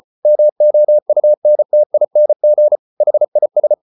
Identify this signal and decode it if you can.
Morse code, heard as a keyed tone, about 34 wpm, 600 Hz. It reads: MOUNTING HIS